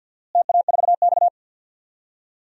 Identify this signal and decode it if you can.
Morse code, heard as a keyed tone, about 51 wpm, 700 Hz. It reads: TA4X
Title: DXCC-100 Call Signs - 4 Characters - Repeated 2x - 50wpm